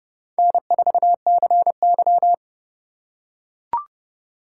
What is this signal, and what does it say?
Morse code, heard as a keyed tone, about 27 wpm, 700 Hz. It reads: N4CY